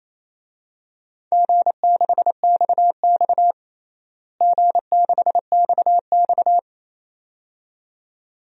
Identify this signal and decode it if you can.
Morse code, heard as a keyed tone, about 28 wpm, 700 Hz. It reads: G6XX G6XX
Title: DXCC-100 Call Signs - 4 Characters - Repeated 2x - 28wpm